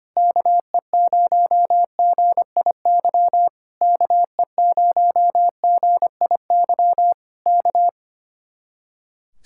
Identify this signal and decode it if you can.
Morse code, heard as a keyed tone, about 24 wpm, 700 Hz. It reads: KE0GIY KE0GIY K
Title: Speed-Racing Common QSO Phrases - 25wpm